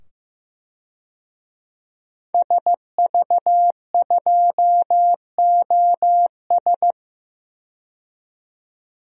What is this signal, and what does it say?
Morse code, heard as a keyed tone, about 15 wpm, 700 Hz. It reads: SV2OS
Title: DXCC-100 Call Signs - 5 Characters - 15wpm